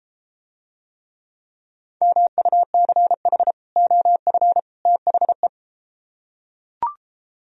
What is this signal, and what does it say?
Morse code, heard as a keyed tone, about 32 wpm, 700 Hz. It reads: MUCH OF THE